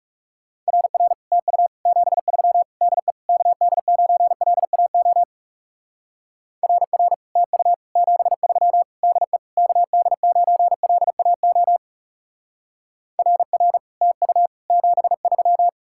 Morse code, heard as a keyed tone, about 37 wpm, 700 Hz. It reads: RR TU 73 DE KD9LAO RR TU 73 DE KD9LAO RR TU 73
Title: Speed-Racing Common QSO Phrases - 30wpm